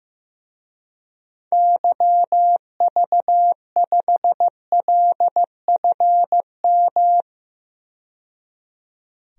Morse code, heard as a keyed tone, about 16 wpm, 700 Hz. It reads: YV5LFM
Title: DXCC-100 Call Signs - 6 Characters - 15wpm